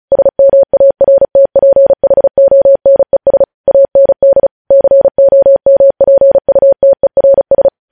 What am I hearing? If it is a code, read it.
Morse code, heard as a keyed tone, about 36 wpm, 550 Hz. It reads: SMARTPHONES AND COMPUTERS
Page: News Headlines in Morse Code at 35 WPM